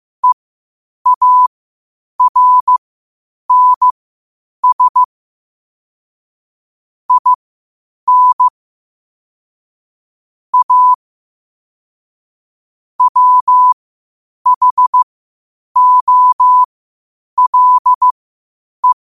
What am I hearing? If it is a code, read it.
Morse code, heard as a keyed tone, about 15 wpm, spaced as for 8 wpm, 1000 Hz. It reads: EARNS IN A WHOLE
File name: qotdfiles / 8wpm